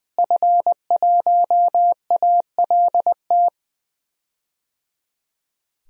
Morse code, heard as a keyed tone, about 20 wpm, 700 Hz. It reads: F1ALT
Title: DXCC-100 Call Signs - 5 Characters - Repeated 3x - 20wpm